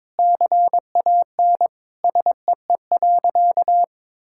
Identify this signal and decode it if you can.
Morse code, heard as a keyed tone, about 22 wpm, 700 Hz. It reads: CAN SEE.
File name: Sentences with 7-Word Phrases - 22wpm